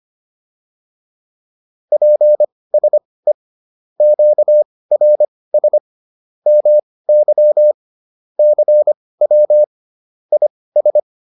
Morse code, heard as a keyed tone, about 25 wpm, spaced as for 18 wpm, 600 Hz. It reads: PSE QRS MY CW IS
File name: QSO408 18